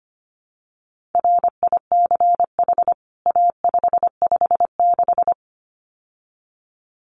Morse code, heard as a keyed tone, about 25 wpm, 700 Hz. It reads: RICH A556
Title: CWT Exchange 25wpm